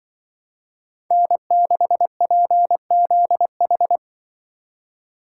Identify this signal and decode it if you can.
Morse code, heard as a keyed tone, about 23 wpm, 700 Hz. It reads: N6PZH
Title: Speed Racing - USA 1x3 Call Signs 20wpm